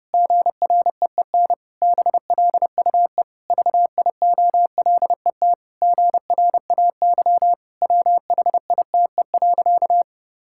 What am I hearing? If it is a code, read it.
Morse code, heard as a keyed tone, about 30 wpm, 700 Hz. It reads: GREEN BLUE VIOLET GRAY WHITE.
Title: Pangram Challenge v2 - 30wpm